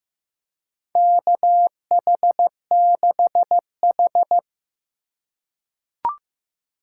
Morse code, heard as a keyed tone, about 15 wpm, 700 Hz. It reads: KH6H
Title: USA Call Signs - 2x Character Spacing - 15wpm